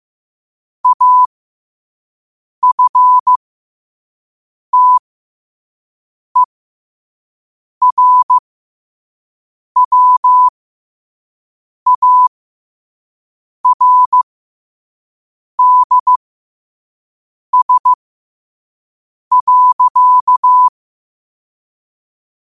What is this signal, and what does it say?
Morse code, heard as a keyed tone, about 15 wpm, spaced as for 5 wpm, 1000 Hz. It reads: AFTERWARDS.